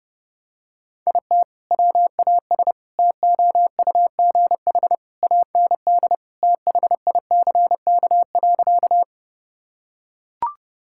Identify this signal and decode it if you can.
Morse code, heard as a keyed tone, about 30 wpm, 700 Hz. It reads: IT WAS TOUGH AND THICK.
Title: Sentences from Top 1600 Words 30wpm